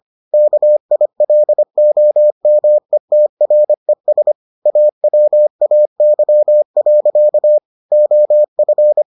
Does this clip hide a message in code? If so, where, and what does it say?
Yes: Morse code from 0.3 to 9.0 s, heard as a keyed tone, about 25 wpm, 600 Hz. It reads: KILOMETRES AWAY. OF